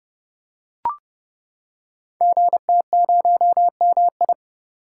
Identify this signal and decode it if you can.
Morse code, heard as a keyed tone, about 28 wpm, 700 Hz. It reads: GT0MI
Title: Code Groups - 5 Letter:Numbers - 30wpm